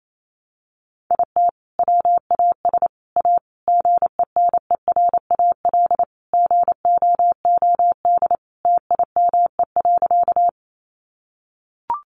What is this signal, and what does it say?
Morse code, heard as a keyed tone, about 28 wpm, 700 Hz. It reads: IT WAS A GENERAL GOOD TIME.
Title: Sentences from Top 600 Words 28wpm